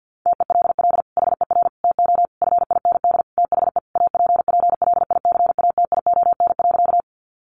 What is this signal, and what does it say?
Morse code, heard as a keyed tone, about 50 wpm, 700 Hz. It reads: TELL HER TO FIND THE APPLICATION.
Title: Sentences from Top 1200 Words 50wpm